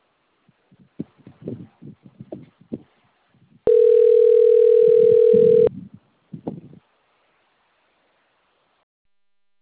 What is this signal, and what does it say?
Ringback tone (North America), 440 + 480 Hz.
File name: out- - - - - .50974